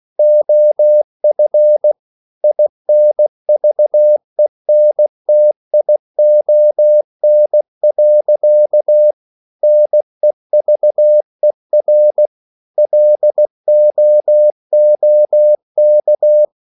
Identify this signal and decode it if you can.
Morse code, heard as a keyed tone, about 16 wpm, 600 Hz. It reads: OF INVENTION. NEVER LOOK
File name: CW / Practice / EnglishSayings16